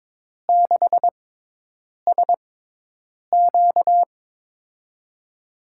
Morse code, heard as a keyed tone, about 22 wpm, spaced as for 8 wpm, 700 Hz. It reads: 6SQ
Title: USA Call Signs - 4 Characters - 6x Character Spacing - 22wpm